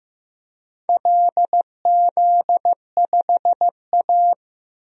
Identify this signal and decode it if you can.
Morse code, heard as a keyed tone, about 15 wpm, 700 Hz. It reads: LZ5A
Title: DXCC-100 Call Signs - 4 Characters - Repeated 3x - 15wpm